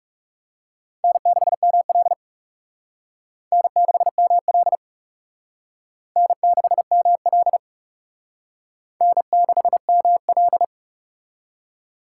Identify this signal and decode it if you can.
Morse code, heard as a keyed tone, about 38 wpm, 700 Hz. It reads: N6ML N6ML N6ML N6ML
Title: Speed Racing - USA 1x2 Call Signs 30wpm